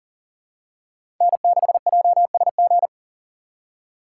Morse code, heard as a keyed tone, about 39 wpm, 700 Hz. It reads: N6JSG
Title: USA Call Signs - 5 Characters - 2x Character Spacing - 40wpm